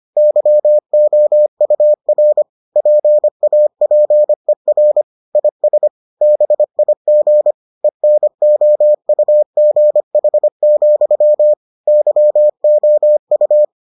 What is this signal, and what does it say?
Morse code, heard as a keyed tone, about 25 wpm, 600 Hz. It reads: YOUR PAPER IS BIG ENOUGH, YOU